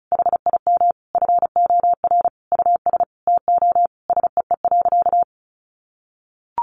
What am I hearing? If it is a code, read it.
Morse code, heard as a keyed tone, about 34 wpm, 700 Hz. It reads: HIM FOR US TO SEE.
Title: Sentences from Top 500 Words 35wpm